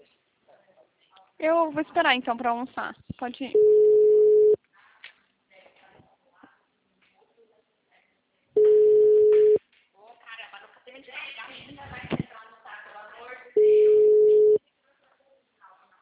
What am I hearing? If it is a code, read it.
Ringback tone (Europe, 425 Hz), with speech.